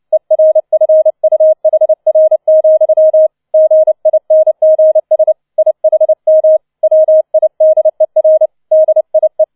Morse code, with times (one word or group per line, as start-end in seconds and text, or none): 0.1-3.3 ERFUHR,
3.5-5.3 GINGS
5.6-6.6 IHM
6.8-8.5 WIDER
8.7-9.4 DIE